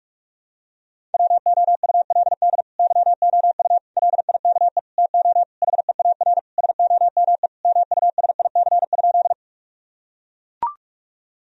Morse code, heard as a keyed tone, about 42 wpm, 700 Hz. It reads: WOULD YOU LIKE TO HEAR SOME MUSIC?
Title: Sentences from Top 1000 Words - Review Part 1 - 45wpm